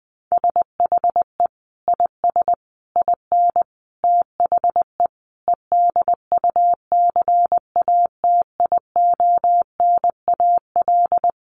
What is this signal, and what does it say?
Morse code, heard as a keyed tone, about 20 wpm, 700 Hz. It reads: SHE IS IN THE EDUCATIONAL
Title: Sentences from Top 1200 Words 20wpm